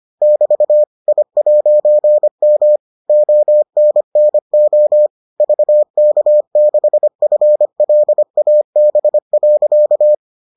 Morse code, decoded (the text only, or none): <BT> I'M ONNO VK6FLAB.